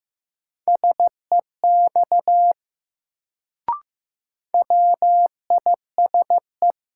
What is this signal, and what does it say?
Morse code, heard as a keyed tone, about 15 wpm, 700 Hz. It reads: SEX WISE
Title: Sentences from Top 1600 Words - New Word Drill - 15wpm